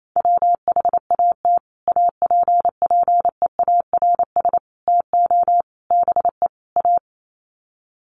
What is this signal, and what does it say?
Morse code, heard as a keyed tone, about 28 wpm, 700 Hz. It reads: WHAT APPEARS TO BE A
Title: Sets of 5 Words Encore2 28wpm